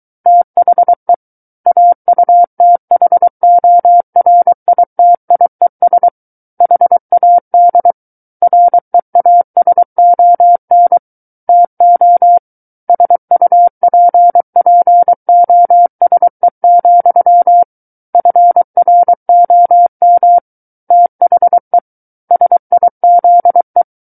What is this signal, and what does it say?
Morse code, heard as a keyed tone, about 23 wpm, 700 Hz. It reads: THE AUTHORITIES HAD REASON TO SUPPOSE, FROM THE SIZE
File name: Book One / 14-Chapter 14 - 23 WPM